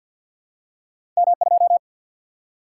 Morse code, heard as a keyed tone, about 54 wpm, 700 Hz. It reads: M2
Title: Single-Letter Single-Number 50wpm